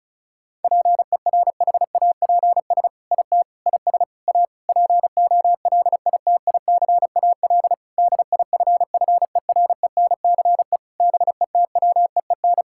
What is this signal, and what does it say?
Morse code, heard as a keyed tone, about 36 wpm, 700 Hz. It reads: PERHAPS IT IS A POLITICAL DIFFERENCE BETWEEN
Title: Sentences from Top 1000 Words - Review Part 5 - 35wpm